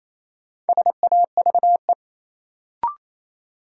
Morse code, heard as a keyed tone, about 28 wpm, 700 Hz. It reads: SAVE
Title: Sentences from Top 1100 Words - New Word Drill - 28wpm